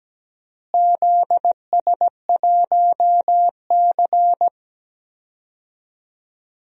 Morse code, heard as a keyed tone, about 17 wpm, 700 Hz. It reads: ZS1C
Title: DXCC-100 Call Signs - 4 Characters - Repeated 3x - 17wpm